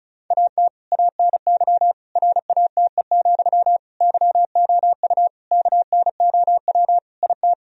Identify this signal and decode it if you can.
Morse code, heard as a keyed tone, about 36 wpm, 700 Hz. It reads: AT ANY RATE, YOU KNOW IT
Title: Sentences from Top 1000 Words - Review Part 4 - 35wpm